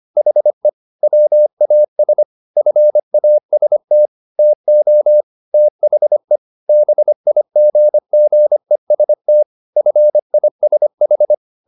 Morse code, heard as a keyed tone, about 25 wpm, 600 Hz. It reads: HE WAS FAST TO THE BIGGEST FISH
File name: the old man and the sea part 1 25wpm